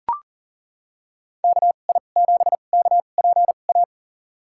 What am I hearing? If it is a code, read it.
Morse code, heard as a keyed tone, about 40 wpm, spaced as for 29 wpm, 700 Hz. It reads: KI7KPA
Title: USA Call Signs - 6 Characters - 2x Character Spacing - 40wpm